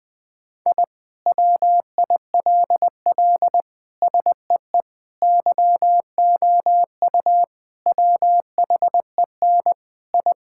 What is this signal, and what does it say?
Morse code, heard as a keyed tone, about 21 wpm, 700 Hz. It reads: I WILL SEE YOU WHEN I
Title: Sentences with 2-Word Phrases - 20wpm